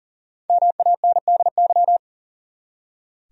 Morse code, heard as a keyed tone, about 37 wpm, 700 Hz. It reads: MANDY
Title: USA Common Names - Max 5 Letters - 40wpm